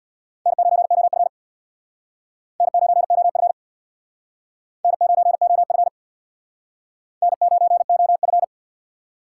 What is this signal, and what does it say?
Morse code, heard as a keyed tone, about 62 wpm, 700 Hz. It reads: N9OF N9OF N9OF N9OF
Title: Speed Racing - USA Call Signs 50wpm